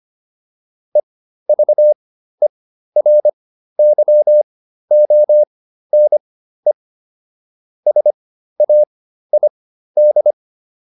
Morse code, heard as a keyed tone, about 25 wpm, spaced as for 13 wpm, 600 Hz. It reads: EVERYONE SAID